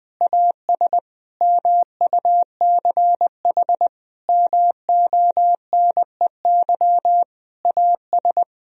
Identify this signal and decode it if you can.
Morse code, heard as a keyed tone, about 20 wpm, 700 Hz. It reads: AS MUCH MONEY AS
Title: Sentences from Top 1000 Words - Review Part 3 - 20wpm